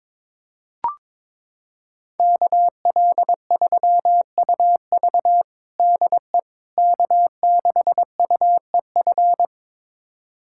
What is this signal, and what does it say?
Morse code, heard as a keyed tone, about 22 wpm, 700 Hz. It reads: KL3UV DE K6UEF
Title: USA Callsign DE Callsign - 5 Characters - 22wpm